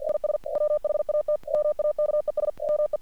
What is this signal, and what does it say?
Morse code, heard as a keyed tone, about 53 wpm, 600 Hz. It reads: IS WHAT MAKES ME
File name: archived / CRSnet